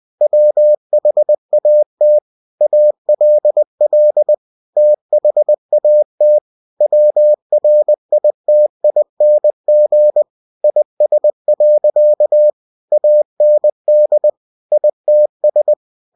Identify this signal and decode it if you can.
Morse code, heard as a keyed tone, about 20 wpm, 600 Hz. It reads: WHAT ALL THAT WRITING IS. AND ITS
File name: cw-books / winnie-the-pooh / winnie-20wpm-page-0057